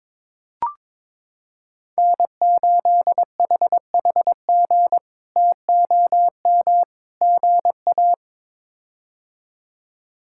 Morse code, heard as a keyed tone, about 22 wpm, 700 Hz. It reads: N8HHG TOM GA
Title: K1USN SST QSO Practice - 22wpm